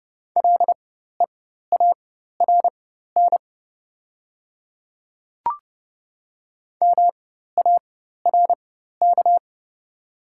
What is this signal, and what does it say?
Morse code, heard as a keyed tone, about 30 wpm, spaced as for 14 wpm, 700 Hz. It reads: LEARN MARK
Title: 4x Character Spacing - Top 300 Words - 30wpm